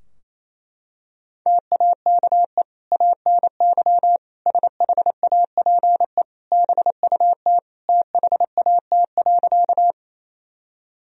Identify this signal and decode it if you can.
Morse code, heard as a keyed tone, about 28 wpm, 700 Hz. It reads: TAKE ANY SHAPE BUT THAT.
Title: Sentences from Top 1000 Words - Review Part 3 - 28wpm